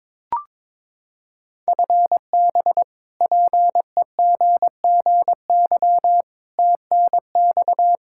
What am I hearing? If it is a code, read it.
Morse code, heard as a keyed tone, about 22 wpm, 700 Hz. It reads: FB PEGGY TNX
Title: Practice QSO - Exchange 3 - More Info - 22wpm